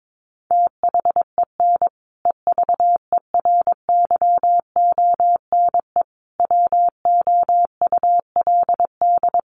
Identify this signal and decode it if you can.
Morse code, heard as a keyed tone, about 22 wpm, 700 Hz. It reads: THEN EVERYONE WOULD
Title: Sentences from Top 1400 Words 22wpm